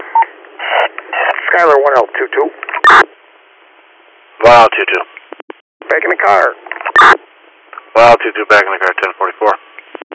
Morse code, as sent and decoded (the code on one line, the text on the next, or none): none